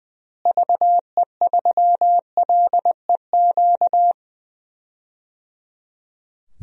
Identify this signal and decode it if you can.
Morse code, heard as a keyed tone, about 20 wpm, 700 Hz. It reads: VE3LEQ
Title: DXCC-100 Call Signs - 6 Characters - Repeated 3x - 20wpm